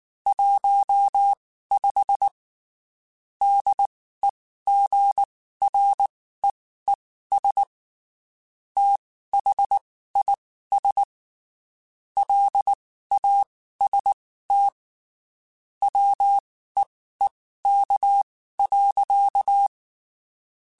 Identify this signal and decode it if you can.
Morse code, heard as a keyed tone, about 19 wpm, spaced as for 14 wpm, 800 Hz. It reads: 15 DEGREES THIS LAST WEEK.
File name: (spaced) / QSOs Spaced-mp3s / WD0FBT @18 sp